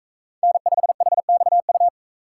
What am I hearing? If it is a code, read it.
Morse code, heard as a keyed tone, about 41 wpm, 700 Hz. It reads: N5HXU